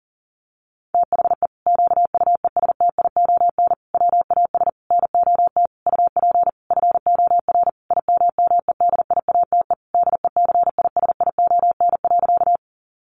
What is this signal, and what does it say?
Morse code, heard as a keyed tone, about 40 wpm, 700 Hz. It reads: THE QUESTION WAS NOT UP FOR IMMEDIATE DECISION.
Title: Sentences from Top 1200 Words 40wpm